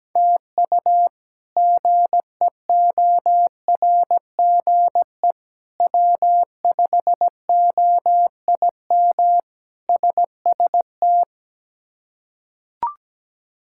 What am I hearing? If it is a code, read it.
Morse code, heard as a keyed tone, about 17 wpm, 700 Hz. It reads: TU GEORGE W5OIM SST
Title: K1USN SST QSO Practice - 17wpm